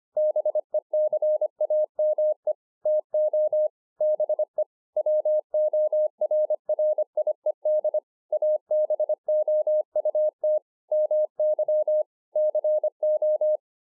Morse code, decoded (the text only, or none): BECAME TO BE WORRIED ABOUT MY CO